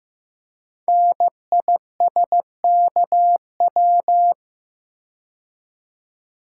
Morse code, heard as a keyed tone, about 15 wpm, 700 Hz. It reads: NISKW
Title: Code Groups - 5 Letters - 15wpm